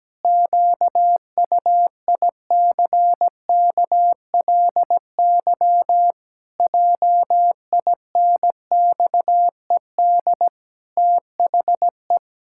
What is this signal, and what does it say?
Morse code, heard as a keyed tone, about 17 wpm, 700 Hz. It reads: QUICKLY JINXED THE
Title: Pangram Challenge v2 - 17wpm